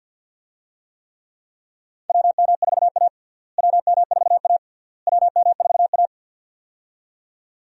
Morse code, heard as a keyed tone, about 52 wpm, 700 Hz. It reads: WM4A WM4A WM4A
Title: USA Call Signs - 4 Characters - Repeated 3x - 50wpm